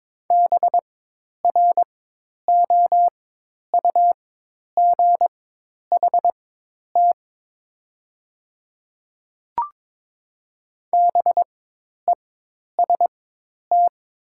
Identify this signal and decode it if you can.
Morse code, heard as a keyed tone, about 23 wpm, spaced as for 10 wpm, 700 Hz. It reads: BROUGHT BEST
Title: 4x Character Spacing - Top 500 Words - 22wpm